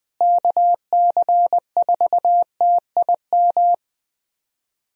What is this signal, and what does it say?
Morse code, heard as a keyed tone, about 20 wpm, 700 Hz. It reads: KC4TIM
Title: USA Call Signs - 6 Characters - Repeated 3x - 20wpm